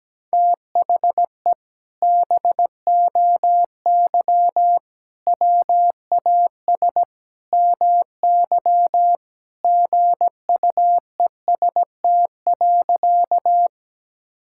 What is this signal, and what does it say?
Morse code, heard as a keyed tone, about 17 wpm, 700 Hz. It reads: THE BOY WAS MY GUEST.
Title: Sentences from Top 1600 Words 17wpm